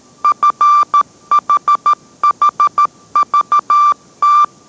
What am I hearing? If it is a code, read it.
Morse code, heard as a keyed tone, about 13 wpm, 1180 Hz. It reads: FHHVT